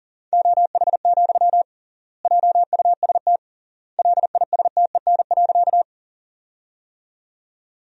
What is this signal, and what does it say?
Morse code, heard as a keyed tone, about 37 wpm, 700 Hz. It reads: OH, JUST LISTEN.
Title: 3x Word-Spacing - Sentences from Top 500 Words - 40wpm